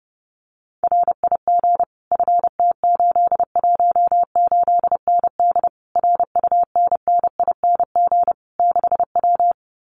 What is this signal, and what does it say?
Morse code, heard as a keyed tone, about 30 wpm, 700 Hz. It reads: RIG FT818ND RUNNING 6W
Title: Common QSO Phrases Encore 30wpm